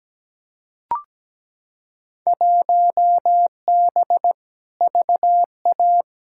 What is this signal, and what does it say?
Morse code, heard as a keyed tone, about 17 wpm, 700 Hz. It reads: 1B VA
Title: Field Day Exchanges - 17wpm